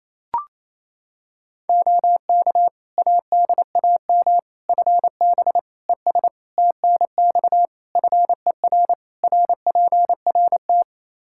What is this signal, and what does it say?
Morse code, heard as a keyed tone, about 28 wpm, 700 Hz. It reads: OK ADAM FB ES TNX FER RPRT
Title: Practice QSO - Exchange 2 - Info - 28wpm